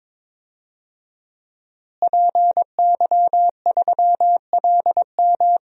Morse code, heard as a keyed tone, about 21 wpm, 700 Hz. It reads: PY3LM
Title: DXCC-100 Call Signs - 5 Characters - Repeated 3x - 22wpm